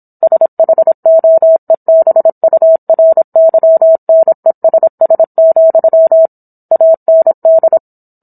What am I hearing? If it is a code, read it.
Morse code, heard as a keyed tone, about 26 wpm, 650 Hz. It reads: SHOEBURYNESS, AND